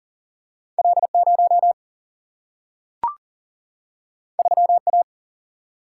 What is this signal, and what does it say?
Morse code, heard as a keyed tone, about 40 wpm, 700 Hz. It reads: L 0 3 A
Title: Single-Letter Single-Number 40wpm